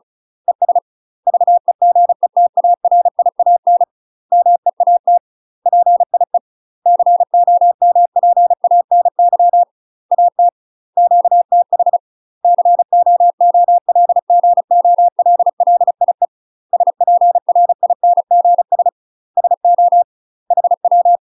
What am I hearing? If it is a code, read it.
Morse code, heard as a keyed tone, about 34 wpm, 700 Hz. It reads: ES VEGETARIAN MEAT PIE COMPANY AT QTH COOLGOLLIE SPRINGS SO HW
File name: UK MASTER QSOs TRAINING / 35WPM / UK MASTER QSOs 0015